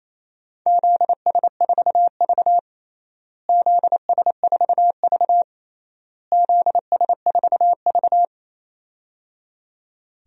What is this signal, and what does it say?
Morse code, heard as a keyed tone, about 28 wpm, 700 Hz. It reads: ZS4V ZS4V ZS4V
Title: DXCC-100 Call Signs - 4 Characters - Repeated 3x - 28wpm